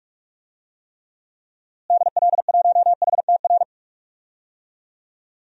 Morse code, heard as a keyed tone, about 46 wpm, 700 Hz. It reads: DL1HTR
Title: DXCC-100 Call Signs - 6 Characters - Repeated 2x - 45wpm